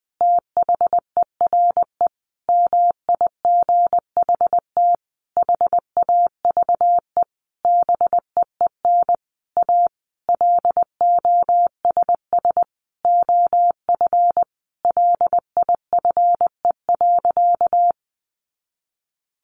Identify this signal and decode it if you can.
Morse code, heard as a keyed tone, about 20 wpm, 700 Hz. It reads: THERE MIGHT HAVE BEEN A LOSS OF LIFE.
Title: Sentences from Top 1400 Words 20wpm